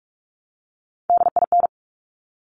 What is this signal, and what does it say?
Morse code, heard as a keyed tone, about 46 wpm, 700 Hz. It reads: DIN
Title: Top Three Letter Combinations in Words 45wpm